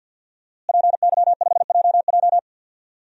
Morse code, heard as a keyed tone, about 47 wpm, 700 Hz. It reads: PY5JJ
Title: DXCC-100 Call Signs - 5 Characters - Repeated 2x - 50wpm